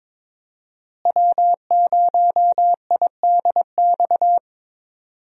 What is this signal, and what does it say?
Morse code, heard as a keyed tone, about 22 wpm, 700 Hz. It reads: W0IDX
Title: USA Call Signs - 5 Characters - Repeated 3x - 22wpm